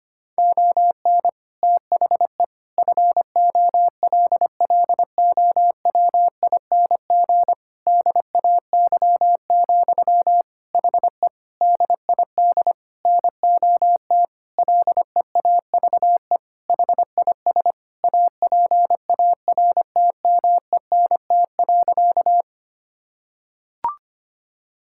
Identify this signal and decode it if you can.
Morse code, heard as a keyed tone, about 25 wpm, 700 Hz. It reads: ON THE FOLLOWING DAY, HE DID NOT LEAVE HIS APARTMENT.
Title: Sentences from Top 1000 Words - Review Part 6 - 25wpm